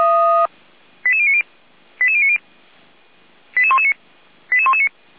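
Five-tone selective call (ZVEI), heard as quick runs of five tones, tones of about 80 ms. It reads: 88090 88090 88119 88119
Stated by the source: ZVEI five-tone call received by SDR, from the Feuerwehr Orlamünde audio server.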